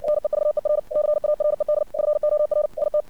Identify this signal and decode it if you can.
Morse code, heard as a keyed tone, about 57 wpm, 600 Hz. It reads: NEVER CARED FOR IT
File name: archived / CRSnet